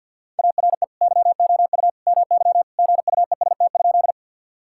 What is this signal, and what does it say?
Morse code, heard as a keyed tone, about 50 wpm, 700 Hz. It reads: ARE YOU MY GUEST?
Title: Sentences from Top 1600 Words 50wpm